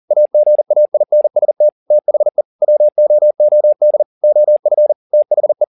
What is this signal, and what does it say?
Morse code, heard as a keyed tone, about 40 wpm, 600 Hz. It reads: AGAINST THE WOOD OF THE